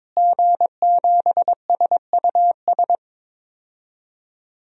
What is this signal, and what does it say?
Morse code, heard as a keyed tone, about 22 wpm, 700 Hz. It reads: G7SUS
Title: DXCC-100 Call Signs - 5 Characters - Repeated 3x - 22wpm